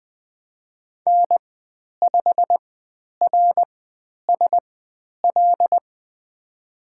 Morse code, heard as a keyed tone, about 20 wpm, spaced as for 10 wpm, 700 Hz. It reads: N5RSL
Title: 102 Lesson 17g - Letter u - Call Signs 20wpm